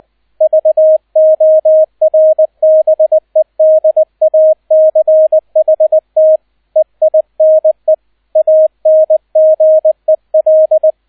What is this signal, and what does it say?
Morse code, heard as a keyed tone, about 20 wpm, 630 Hz, with radio noise. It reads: VORBEDACHT EINE ANGEL
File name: 002 Zweiter Streich 20wpm